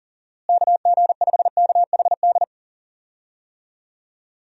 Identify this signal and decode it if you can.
Morse code, heard as a keyed tone, about 40 wpm, 700 Hz. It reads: KG5KHD
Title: USA Call Signs - 6 Characters - Repeated 3x - 40wpm